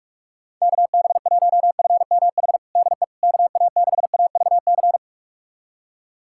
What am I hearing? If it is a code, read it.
Morse code, heard as a keyed tone, about 41 wpm, 700 Hz. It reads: KB1FMH DE KA6AVC
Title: USA Callsign DE Callsign - 6 Characters - 45wpm